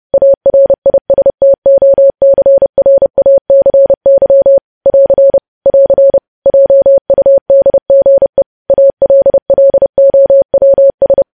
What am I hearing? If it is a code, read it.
Morse code, heard as a keyed tone, about 30 wpm, 550 Hz. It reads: ARISTOCRACY <AR> <AR> JUDGE ALLOWS
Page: news headlines in morse code at 30 wpm